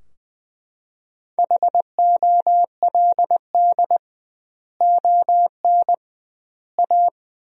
Morse code, heard as a keyed tone, about 20 wpm, 700 Hz. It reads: HOLD ON A
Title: 2x Word-Spacing - Sentences from Top 500 Words - 20wpm